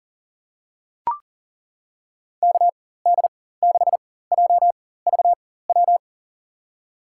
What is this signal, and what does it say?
Morse code, heard as a keyed tone, about 40 wpm, spaced as for 19 wpm, 700 Hz. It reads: KD6JVW
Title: USA Call Signs - 4x Character Spacing - 40wpm